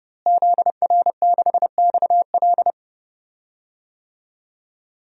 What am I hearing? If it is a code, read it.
Morse code, heard as a keyed tone, about 30 wpm, 700 Hz. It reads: ZR6XL
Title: DXCC-100 Call Signs - 5 Characters - Repeated 2x - 30wpm